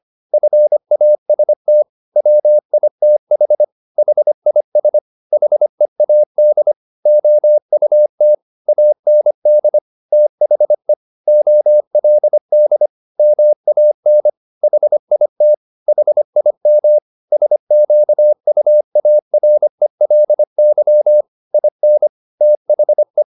Morse code, heard as a keyed tone, about 25 wpm, 600 Hz. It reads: FAST WITH HIS HEAD OUT AND THE OLD MAN HIT HIM SQUARELY IN THE